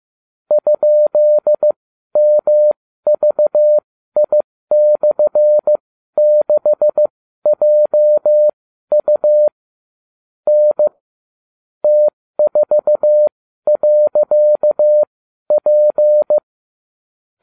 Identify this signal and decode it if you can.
Morse code, heard as a keyed tone, about 15 wpm, spaced as for 12 wpm, 600 Hz. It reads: ?MVI/6JU N T4.P